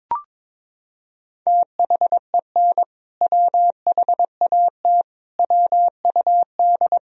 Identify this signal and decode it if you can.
Morse code, heard as a keyed tone, about 21 wpm, 700 Hz. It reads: THEN WHAT WUD
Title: Sentences from Top 400 Words and Abbreviations 22wpm